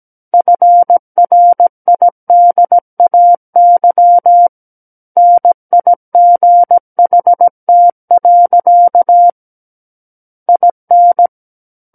Morse code, heard as a keyed tone, about 17 wpm, 700 Hz. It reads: FRIDAY NIGHT. IN
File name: Book One / 08-Chapter 8 - 17 WPM